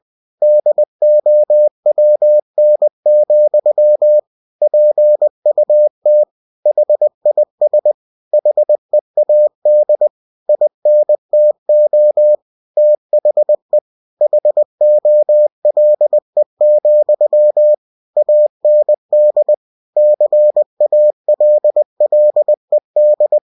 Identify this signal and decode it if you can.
Morse code, heard as a keyed tone, about 20 wpm, 600 Hz. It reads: DOWN, PUT HIS HEAD INTO THE HOLE, AND CALLED
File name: cw-books / winnie-the-pooh / winnie-20wpm-page-0013